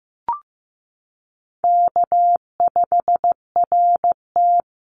Morse code, heard as a keyed tone, about 15 wpm, 700 Hz. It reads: K5RT
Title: Call Signs Repeated 3x - 15wpm